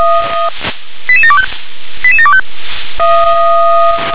Five-tone selective call (ZVEI), heard as quick runs of five tones, tones of about 70 ms. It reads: 88725 88725